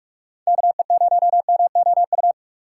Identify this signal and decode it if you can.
Morse code, heard as a keyed tone, about 47 wpm, 700 Hz. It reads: KE0MOU
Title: USA Call Signs - 6 Characters - 3x Character Spacing - 45wpm